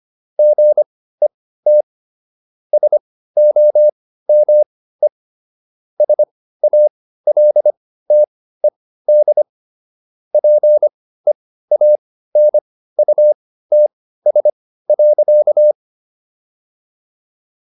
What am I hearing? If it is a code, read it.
Morse code, heard as a keyed tone, about 25 wpm, spaced as for 15 wpm, 600 Hz. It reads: GET SOME SALTED PEANUTS.